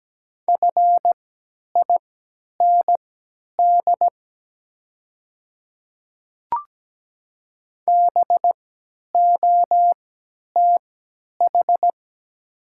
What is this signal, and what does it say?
Morse code, heard as a keyed tone, about 17 wpm, spaced as for 10 wpm, 700 Hz. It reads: FIND BOTH
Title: 3x Character Spacing - Top 500 Words - 17wpm